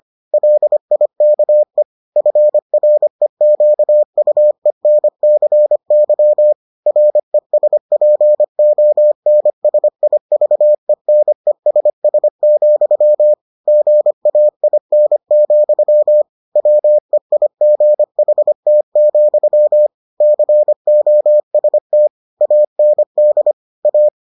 Morse code, heard as a keyed tone, about 25 wpm, 600 Hz. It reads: LIKE FREQUENCY RESPONSIVENESS, GAIN, WEIGHT, COST AND A